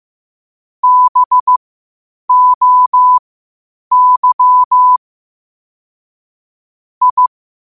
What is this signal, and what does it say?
Morse code, heard as a keyed tone, about 15 wpm, spaced as for 8 wpm, 1000 Hz. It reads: BOY I
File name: qotdfiles / 8wpm